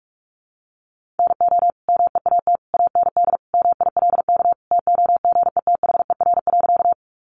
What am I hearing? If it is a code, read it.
Morse code, heard as a keyed tone, about 44 wpm, 700 Hz. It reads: NO MEAT AND MILK TOGETHER.
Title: Sentences from Top 1500 Words 45wpm